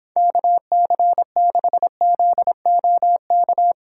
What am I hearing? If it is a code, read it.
Morse code, heard as a keyed tone, about 31 wpm, 700 Hz. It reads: KC6ZOK